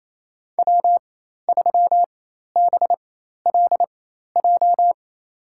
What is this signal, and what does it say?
Morse code, heard as a keyed tone, about 28 wpm, spaced as for 13 wpm, 700 Hz. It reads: W3BLJ
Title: USA Call Signs - 5 Characters - 4x Character Spacing - 28wpm